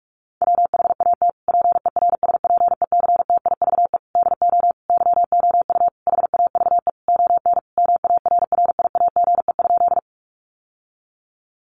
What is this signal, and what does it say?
Morse code, heard as a keyed tone, about 44 wpm, 700 Hz. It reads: WHAT PERSPECTIVE DO YOU HAVE ON MARRIAGE?
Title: Sentences from Top 1200 Words 45wpm